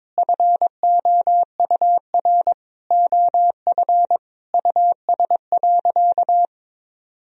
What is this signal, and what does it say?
Morse code, heard as a keyed tone, about 22 wpm, 700 Hz. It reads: FOUR OF US.
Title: Sentences from Top 300 Words and Abbreviations 22wpm